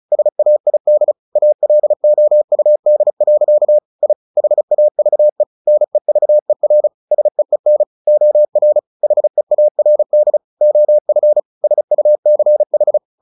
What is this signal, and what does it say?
Morse code, heard as a keyed tone, about 34 wpm, 600 Hz. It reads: SAID ALOUD. I HAVE NEVER SEEN OR HEARD OF SUCH
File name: the old man and the sea 2 35wpm